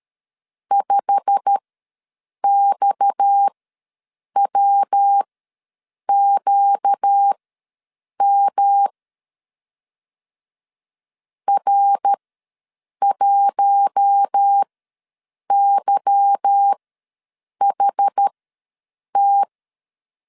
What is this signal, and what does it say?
Morse code, heard as a keyed tone, about 13 wpm, spaced as for 7 wpm, 800 Hz. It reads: 5XWQM R1YHT
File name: LEZIONI / 8 YJ45 / 8 YJ45 ws 12wpm 3146